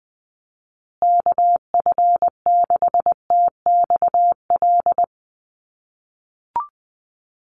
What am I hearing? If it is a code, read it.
Morse code, heard as a keyed tone, about 20 wpm, 700 Hz. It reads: KF6TXL
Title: USA Call Signs Repeated 3x 20wpm